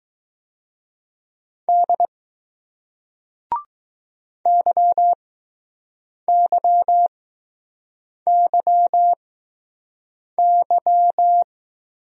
Morse code, heard as a keyed tone, about 20 wpm, spaced as for 6 wpm, 700 Hz. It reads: D YYYY